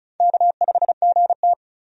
Morse code, heard as a keyed tone, about 36 wpm, 700 Hz. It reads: K5GT
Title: USA Call Signs - 4 Characters - 35wpm